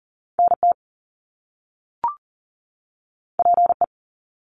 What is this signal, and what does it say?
Morse code, heard as a keyed tone, about 40 wpm, 700 Hz. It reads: NT PE
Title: Canadian Provinces and Territories 40wpm